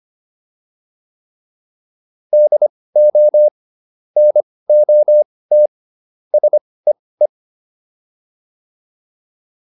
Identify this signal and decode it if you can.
Morse code, heard as a keyed tone, about 25 wpm, spaced as for 18 wpm, 600 Hz. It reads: DO NOT SEE
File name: WD204 18